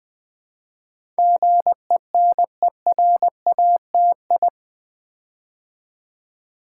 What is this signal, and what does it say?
Morse code, heard as a keyed tone, about 20 wpm, 700 Hz. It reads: GENERATI
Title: Build A Word - 4 Letters At A Time - 10-Letter Words - 20wpm